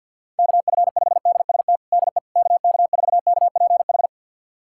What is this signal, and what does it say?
Morse code, heard as a keyed tone, about 52 wpm, 700 Hz. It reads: KF5DST DE KK4KPH
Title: USA Callsign DE Callsign 50wpm